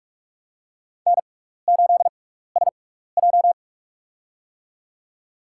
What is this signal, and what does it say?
Morse code, heard as a keyed tone, about 46 wpm, spaced as for 15 wpm, 700 Hz. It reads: N8SJ
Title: USA Call Signs - 4 Characters - 6x Character Spacing - 45wpm